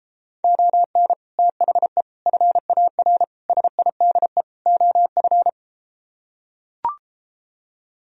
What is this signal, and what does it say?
Morse code, heard as a keyed tone, about 34 wpm, 700 Hz. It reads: ON THE FAR SIDE OF